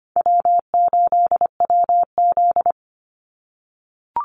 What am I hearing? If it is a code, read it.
Morse code, heard as a keyed tone, about 25 wpm, 700 Hz. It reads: W8WZ
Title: Call Signs - 25wpm